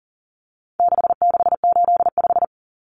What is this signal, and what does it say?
Morse code, heard as a keyed tone, about 40 wpm, 700 Hz. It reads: 6685
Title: Common Cut Numbers - 4-Digit 40wpm